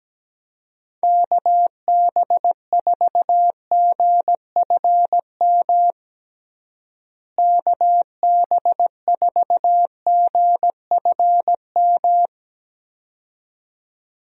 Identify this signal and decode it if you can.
Morse code, heard as a keyed tone, about 17 wpm, 700 Hz. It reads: KB4GFM KB4GFM
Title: USA Call Signs - 6 Characters - Repeated 2x - 17wpm